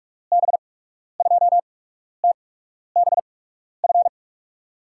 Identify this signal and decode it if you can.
Morse code, heard as a keyed tone, about 46 wpm, spaced as for 12 wpm, 700 Hz. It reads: B2TBF
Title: USA Call Signs - 6 Characters - 8x Character Spacing - 45wpm